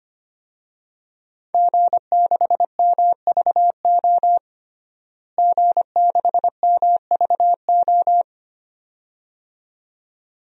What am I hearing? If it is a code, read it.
Morse code, heard as a keyed tone, about 25 wpm, 700 Hz. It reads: G6MVO G6MVO
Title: DXCC-100 Call Signs - 5 Characters - Repeated 2x - 25wpm